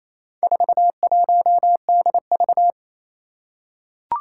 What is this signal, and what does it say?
Morse code, heard as a keyed tone, about 28 wpm, 700 Hz. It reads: 41DV
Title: Four Challenging Letter-Number Combinations 28wpm